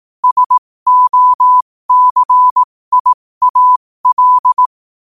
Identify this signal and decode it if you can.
Morse code, heard as a keyed tone, about 18 wpm, spaced as for 15 wpm, 1000 Hz. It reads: SOCIAL
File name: 15wpm